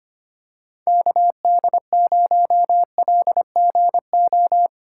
Morse code, heard as a keyed tone, about 25 wpm, 700 Hz. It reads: KD0LGO
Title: USA Call Signs - 4x Character Spacing - 25wpm